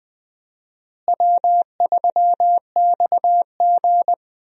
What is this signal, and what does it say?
Morse code, heard as a keyed tone, about 20 wpm, 700 Hz. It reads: W3XG
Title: USA Call Signs - 4 Characters - Repeated 3x - 20wpm